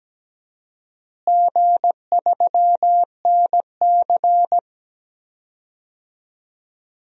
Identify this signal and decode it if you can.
Morse code, heard as a keyed tone, about 17 wpm, 700 Hz. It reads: G3NC
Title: DXCC-100 Call Signs - 4 Characters - Repeated 3x - 17wpm